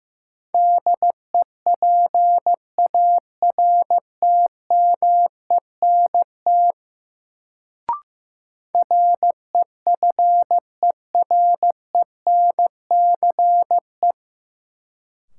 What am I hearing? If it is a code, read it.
Morse code, heard as a keyed tone, about 15 wpm, 700 Hz. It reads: DEPARTMENT REFERENCE
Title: Sentences from Top 1200 Words - New Word Drill Alt - 15wpm